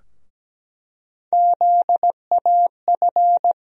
Morse code, heard as a keyed tone, about 17 wpm, 700 Hz. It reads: ZAF
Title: USA Call Sign Postfixes - 3 Letters 17wpm